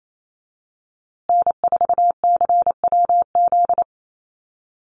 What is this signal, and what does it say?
Morse code, heard as a keyed tone, about 28 wpm, 700 Hz. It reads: N4CWZ
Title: USA Call Signs Repeated 3x 28wpm